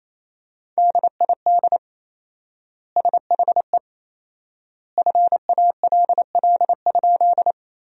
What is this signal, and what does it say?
Morse code, heard as a keyed tone, about 28 wpm, 700 Hz. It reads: DID SHE FALL?
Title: 4x Word-Spacing - Sentences from Top 500 Words - 28wpm